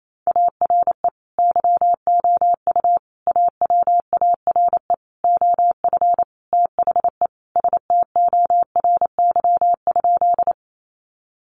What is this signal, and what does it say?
Morse code, heard as a keyed tone, about 28 wpm, 700 Hz. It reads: ARE YOU AWARE OF THE STORY?
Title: Sentences from Top 1200 Words 28wpm